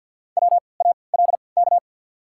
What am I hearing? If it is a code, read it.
Morse code, heard as a keyed tone, about 53 wpm, spaced as for 29 wpm, 690 Hz. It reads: WALK
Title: 3x Character Spacing - Top 300 Words - 50wpm